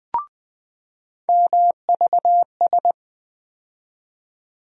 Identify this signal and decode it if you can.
Morse code, heard as a keyed tone, about 20 wpm, 700 Hz. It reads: MVS
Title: USA Call Sign Postfixes - 3 Letters 20wpm